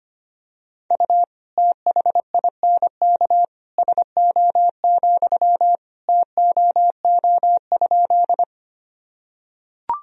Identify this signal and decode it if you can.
Morse code, heard as a keyed tone, about 25 wpm, 700 Hz. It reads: U THINK SO, TOO?
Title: Sentences from Top 200 Words and Abbreviations 25wpm